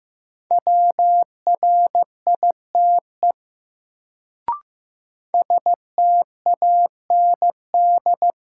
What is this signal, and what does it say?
Morse code, heard as a keyed tone, about 15 wpm, 700 Hz. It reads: WRITE STAND
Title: Sentences from Top 300 Words - New Word Drill - 15wpm